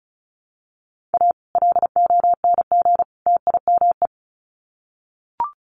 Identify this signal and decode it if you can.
Morse code, heard as a keyed tone, about 34 wpm, 700 Hz. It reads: A LONG TIME
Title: Sets of 3 Words 35wpm